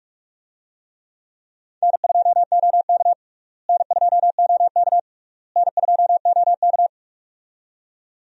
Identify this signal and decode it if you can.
Morse code, heard as a keyed tone, about 47 wpm, 700 Hz. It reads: N2OK N2OK N2OK
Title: USA Call Signs - 4 Characters - Repeated 3x - 45wpm